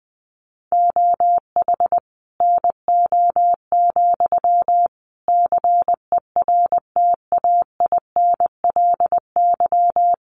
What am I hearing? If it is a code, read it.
Morse code, heard as a keyed tone, about 20 wpm, 700 Hz. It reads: OH NO, CERTAINLY
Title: Sentences from Top 900 Words 20wpm